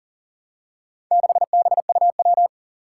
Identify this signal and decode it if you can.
Morse code, heard as a keyed tone, about 42 wpm, 700 Hz. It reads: 6BUW
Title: Four Challenging Letter-Number Combinations 40wpm